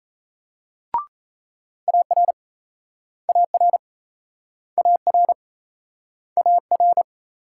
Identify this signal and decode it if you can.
Morse code, heard as a keyed tone, about 38 wpm, 700 Hz. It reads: AR AR AR AR